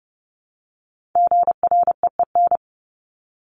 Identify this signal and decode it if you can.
Morse code, heard as a keyed tone, about 30 wpm, 700 Hz. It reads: GREEN
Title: Top 500 Words 30wpm